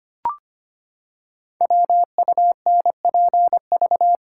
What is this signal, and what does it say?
Morse code, heard as a keyed tone, about 25 wpm, 700 Hz. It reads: WUNPV
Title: Code Groups - 5 Letters - 25wpm